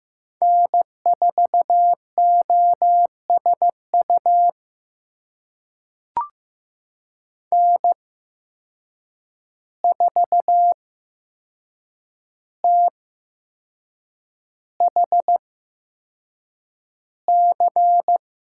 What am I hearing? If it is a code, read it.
Morse code, heard as a keyed tone, about 15 wpm, spaced as for 6 wpm, 700 Hz. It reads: N4OSU N4THC
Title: USA Call Signs - 5 Characters - 8x Character Spacing - 15wpm